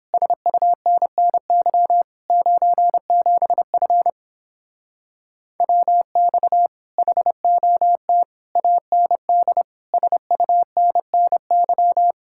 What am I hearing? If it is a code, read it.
Morse code, heard as a keyed tone, about 29 wpm, 700 Hz. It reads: SUNNY 97F WX HOT AND SUNNY